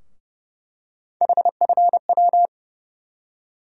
Morse code, heard as a keyed tone, about 32 wpm, 700 Hz. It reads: HFW
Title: USA Call Sign Postfixes - 3 Letters 30wpm